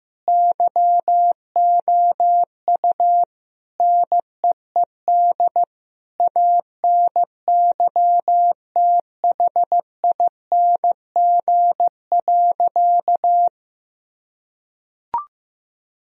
Morse code, heard as a keyed tone, about 15 wpm, 700 Hz. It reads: YOU NEED ANYTHING.
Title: Sentences from Top 1000 Words - Review Part 1 - 15wpm